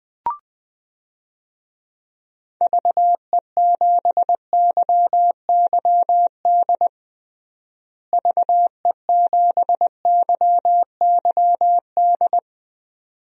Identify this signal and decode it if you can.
Morse code, heard as a keyed tone, about 20 wpm, 700 Hz. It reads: VE7YYD VE7YYD
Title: DXCC-100 Call Signs - 6 Characters - Repeated 2x - 20wpm